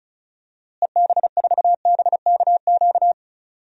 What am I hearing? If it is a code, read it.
Morse code, heard as a keyed tone, about 36 wpm, 700 Hz. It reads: EB4BKQ
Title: DXCC-100 Call Signs - 6 Characters - Repeated 2x - 35wpm